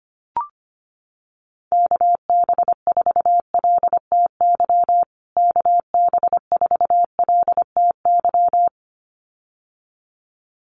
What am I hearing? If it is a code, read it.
Morse code, heard as a keyed tone, about 25 wpm, 700 Hz. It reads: KB4LTY KB4LTY
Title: USA Call Signs Repeated 2x 25wpm